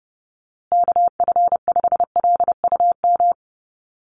Morse code, heard as a keyed tone, about 30 wpm, 700 Hz. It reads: KF5LUM
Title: USA Call Signs Repeated 2x 30wpm